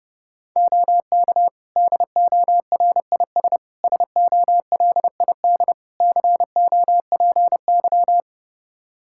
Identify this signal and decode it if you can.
Morse code, heard as a keyed tone, about 30 wpm, 700 Hz. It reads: OK DORIS SOLID COPY
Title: Practice QSO - Exchange 3 - More Info - 30wpm